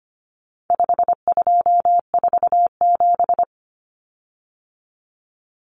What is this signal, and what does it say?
Morse code, heard as a keyed tone, about 25 wpm, 700 Hz. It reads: 5247
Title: Common Cut Numbers - 4-Digit 25wpm